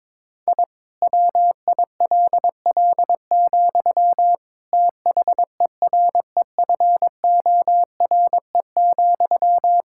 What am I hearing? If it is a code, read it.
Morse code, heard as a keyed tone, about 22 wpm, 700 Hz. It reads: I WILL, THEREFORE,
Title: Sentences from Top 1000 Words - Review Part 5 - 22wpm